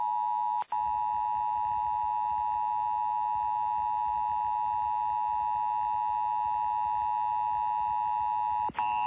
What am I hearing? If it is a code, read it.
Emergency Alert System (SAME) tones (the 853 and 960 Hz attention tone). It is heard from 0.7 to 8.7 s.
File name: eas / KKPRKXPN / KKPRKXPN